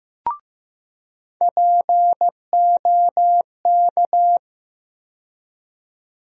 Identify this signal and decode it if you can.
Morse code, heard as a keyed tone, about 16 wpm, 700 Hz. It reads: POK
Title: USA Call Sign Postfixes - 3 Letters 15wpm